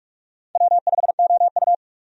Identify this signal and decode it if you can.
Morse code, heard as a keyed tone, about 46 wpm, 700 Hz. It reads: W5OU
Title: USA Call Signs - 4 Characters - 45wpm